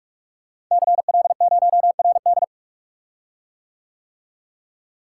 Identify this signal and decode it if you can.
Morse code, heard as a keyed tone, about 44 wpm, 700 Hz. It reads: CL0RD
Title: Code Groups - 5 Letter:Numbers - 45wpm